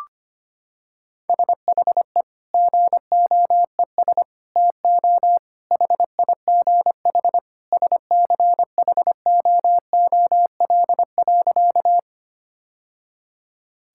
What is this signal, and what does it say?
Morse code, heard as a keyed tone, about 25 wpm, 700 Hz. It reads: SHE GOES TO HIGH SCHOOL.
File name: practice / 25wpm / Sentences with 2-Word Phrases - 25wpm